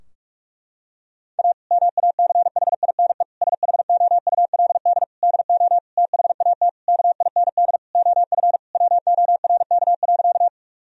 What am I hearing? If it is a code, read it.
Morse code, heard as a keyed tone, about 41 wpm, 700 Hz. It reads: A MACHINE SHOULD DO THAT KIND OF WORK.
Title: Sentences from Top 1000 Words - Review Part 2 - 45wpm